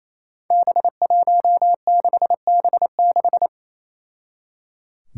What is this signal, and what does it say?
Morse code, heard as a keyed tone, about 27 wpm, 700 Hz. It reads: B16B6
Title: Five Challenging Letter-Number Combinations 28wpm